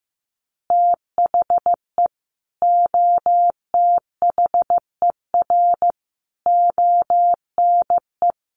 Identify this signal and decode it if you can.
Morse code, heard as a keyed tone, about 15 wpm, 700 Hz. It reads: THE OTHER ONE
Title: Additional Easy Phrases 15wpm